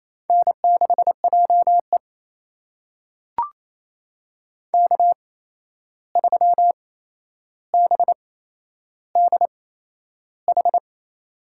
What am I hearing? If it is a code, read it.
Morse code, heard as a keyed tone, about 28 wpm, spaced as for 11 wpm, 700 Hz. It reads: N6JE K3BDH
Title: USA Call Signs - 8x Character Spacing - 28wpm